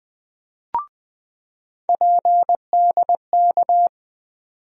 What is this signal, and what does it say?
Morse code, heard as a keyed tone, about 20 wpm, 700 Hz. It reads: PDK
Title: USA Call Sign Postfixes - 3 Letters 20wpm